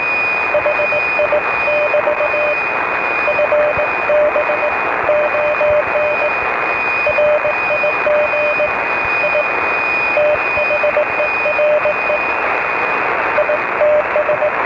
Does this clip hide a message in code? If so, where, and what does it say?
Yes: Morse code from 0.5 to 14.5 s, heard as a keyed tone, about 19 wpm, 620 Hz, with radio noise. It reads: HI <BT> FB ON RIG I THERE ITS